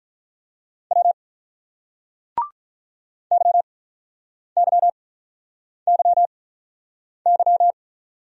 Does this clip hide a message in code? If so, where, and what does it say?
Yes: Morse code from 0.9 to 7.7 s, heard as a keyed tone, about 47 wpm, spaced as for 9 wpm, 700 Hz. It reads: W YYYY